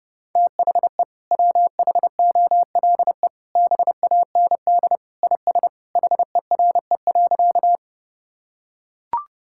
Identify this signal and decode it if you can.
Morse code, heard as a keyed tone, about 30 wpm, 700 Hz. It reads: THE WHOLE BAND IS HERE.
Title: Sentences from Top 1600 Words 30wpm